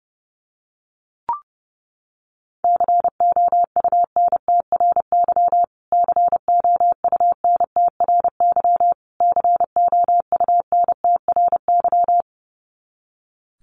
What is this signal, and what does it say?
Morse code, heard as a keyed tone, about 30 wpm, 700 Hz. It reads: COUNTRY COUNTRY COUNTRY
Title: Top 500 Words Repeated 30wpm